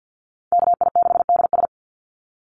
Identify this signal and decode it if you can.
Morse code, heard as a keyed tone, about 49 wpm, 700 Hz. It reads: KI6DS
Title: NA SOTA Chaser Callsigns 50wpm